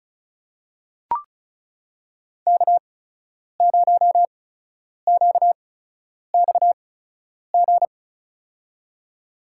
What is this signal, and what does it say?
Morse code, heard as a keyed tone, about 34 wpm, spaced as for 10 wpm, 700 Hz. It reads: K0QXG
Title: USA Call Signs - 5 Characters - 8x Character Spacing - 35wpm